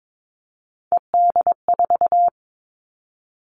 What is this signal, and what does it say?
Morse code, heard as a keyed tone, about 22 wpm, 700 Hz. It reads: ED4
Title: All Cut Numbers - 3-Digit 22wpm